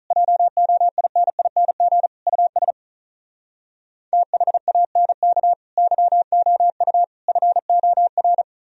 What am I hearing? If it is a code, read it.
Morse code, heard as a keyed tone, about 46 wpm, 700 Hz. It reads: JOINING US THANK YOU FOR